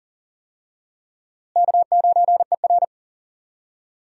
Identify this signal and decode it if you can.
Morse code, heard as a keyed tone, about 37 wpm, 700 Hz. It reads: K9ER
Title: USA Call Signs - 4 Characters - Repeated 3x - 40wpm